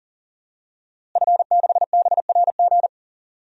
Morse code, heard as a keyed tone, about 40 wpm, 700 Hz. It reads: F6BRG
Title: DXCC-100 Call Signs - 5 Characters - Repeated 2x - 40wpm